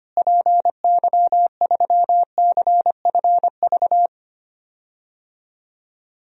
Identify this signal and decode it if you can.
Morse code, heard as a keyed tone, about 25 wpm, 700 Hz. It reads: PY3CFV
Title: DXCC-100 Call Signs - 6 Characters - Repeated 2x - 25wpm